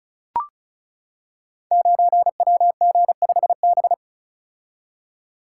Morse code, heard as a keyed tone, about 36 wpm, 700 Hz. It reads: 9WG5B
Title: Code Groups - 5 Letter:Numbers - 35wpm